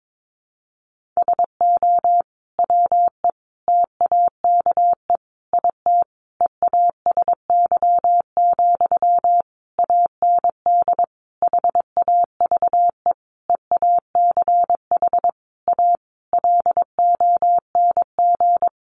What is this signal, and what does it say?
Morse code, heard as a keyed tone, about 22 wpm, 700 Hz. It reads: SO WE TAKE IT EASY, AND HAVE EACH A LONG
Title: Sentences from Top 700 Words 22wpm